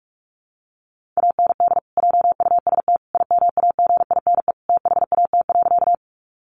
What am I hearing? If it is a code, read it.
Morse code, heard as a keyed tone, about 44 wpm, 700 Hz. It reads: AND JUST IMAGINE THAT.
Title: Sentences from Top 1400 Words 45wpm